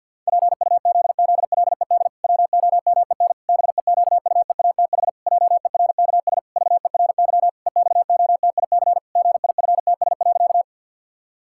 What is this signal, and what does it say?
Morse code, heard as a keyed tone, about 50 wpm, 700 Hz. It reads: PUZZLED WOMEN BEQUEATH JERKS VERY EXOTIC GIFTS.
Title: Pangram Challenge v2 - 50wpm